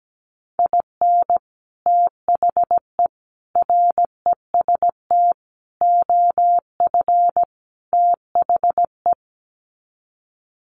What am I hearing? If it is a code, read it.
Morse code, heard as a keyed tone, about 17 wpm, 700 Hz. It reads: IN THE REST OF THE
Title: Sets of 5 Words 17wpm